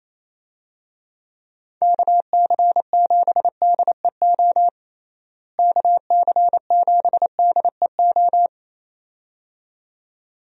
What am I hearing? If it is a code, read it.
Morse code, heard as a keyed tone, about 28 wpm, 700 Hz. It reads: KC7DEO KC7DEO
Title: USA Call Signs - 6 Characters - Repeated 2x - 28wpm